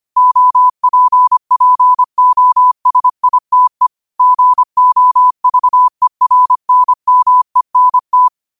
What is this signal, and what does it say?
Morse code, heard as a keyed tone, about 25 wpm, 1000 Hz. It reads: OPPOSITE GOVERNMENT
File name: qotdfiles / 25wpm